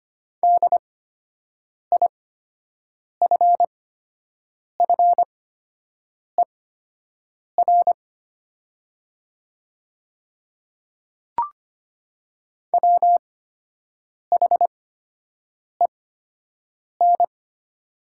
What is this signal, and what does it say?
Morse code, heard as a keyed tone, about 25 wpm, spaced as for 7 wpm, 700 Hz. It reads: DIFFER WHEN
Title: 8x Character Spacing - Top 200 Words - 25wpm